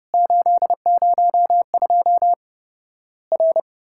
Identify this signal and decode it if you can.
Morse code, heard as a keyed tone, about 32 wpm, 700 Hz. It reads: 802 R
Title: Worked All Europe DX Contest - QTC Exchange - 30wpm